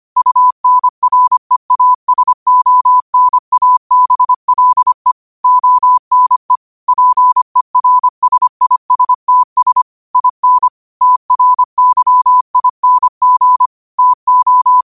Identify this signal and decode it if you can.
Morse code, heard as a keyed tone, about 25 wpm, 1000 Hz. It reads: UNREASONABLE ONE PERSISTS IN TRYING TO
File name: qotdfiles / 25wpm